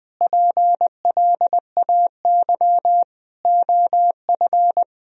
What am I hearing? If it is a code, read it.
Morse code, heard as a keyed tone, about 21 wpm, 700 Hz. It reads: PLAY OF
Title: Sentences from Top 1000 Words - Review Part 4 - 20wpm